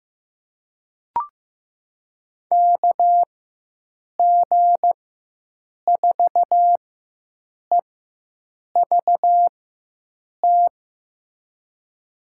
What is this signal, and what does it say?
Morse code, heard as a keyed tone, about 15 wpm, spaced as for 7 wpm, 700 Hz. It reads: KG4EVT
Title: USA Call Signs - 6 Characters - 4x Character Spacing - 15wpm